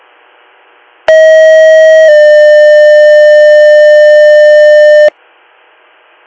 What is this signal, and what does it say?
Two-tone page, heard as a one-second tone, then a three-second tone, about 640 then 600 Hz. It is heard from 1.1 to 5.1 s.